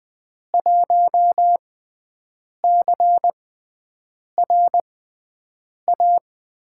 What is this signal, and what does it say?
Morse code, heard as a keyed tone, about 20 wpm, spaced as for 7 wpm, 700 Hz. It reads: 1CRA
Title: USA Call Signs - 6 Characters - 6x Character Spacing - 20wpm